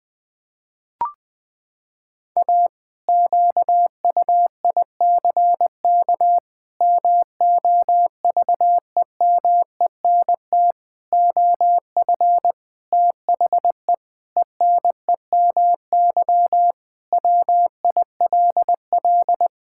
Morse code, heard as a keyed tone, about 20 wpm, 700 Hz. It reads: A QUICK MOVEMENT OF THE ENEMY WILL
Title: Pangram Challenge v2 - 20wpm